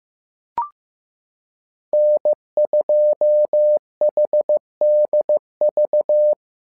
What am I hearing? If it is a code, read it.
Morse code, heard as a keyed tone, about 15 wpm, 600 Hz. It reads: N2HDV
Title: SOTA QSO Practice 15wpm